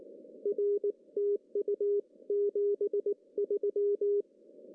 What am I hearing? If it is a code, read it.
Morse code, heard as a keyed tone, about 19 wpm, 440 Hz. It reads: R TU 73
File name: CW / cqww-cw-k5tr